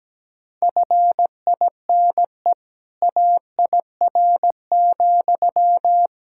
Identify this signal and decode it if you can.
Morse code, heard as a keyed tone, about 17 wpm, 700 Hz. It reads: FINE AIR,
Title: Sentences from Top 1600 Words 17wpm